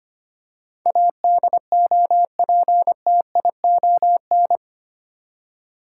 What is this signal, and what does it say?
Morse code, heard as a keyed tone, about 25 wpm, 700 Hz. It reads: ADOPTION
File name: Eight Letter Words - 25wpm